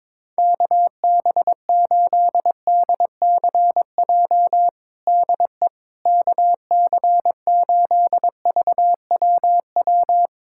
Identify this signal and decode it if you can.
Morse code, heard as a keyed tone, about 22 wpm, 700 Hz. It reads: KB8DCJ DE KC8VWW
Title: USA Callsign DE Callsign - 6 Characters - 22wpm